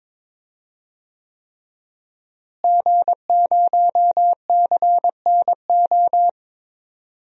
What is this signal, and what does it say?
Morse code, heard as a keyed tone, about 22 wpm, 700 Hz. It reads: G0CNO
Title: DXCC-100 Call Signs - 5 Characters - Repeated 3x - 22wpm